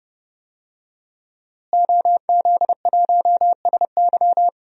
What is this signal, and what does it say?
Morse code, heard as a keyed tone, about 29 wpm, 700 Hz. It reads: OZ1SY
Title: DXCC-100 Call Signs - 5 Characters - Repeated 2x - 30wpm